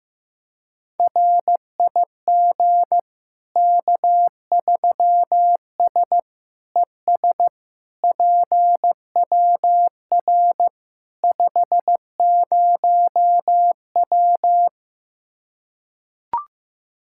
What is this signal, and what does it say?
Morse code, heard as a keyed tone, about 15 wpm, 700 Hz. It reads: RIG K3S ES PWR 50W
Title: Practice QSO - Exchange 2 - Info - 15wpm